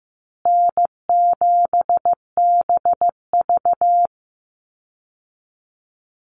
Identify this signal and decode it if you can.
Morse code, heard as a keyed tone, about 15 wpm, 700 Hz. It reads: N7BV
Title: Easy Call Signs Repeated 2x - 15wpm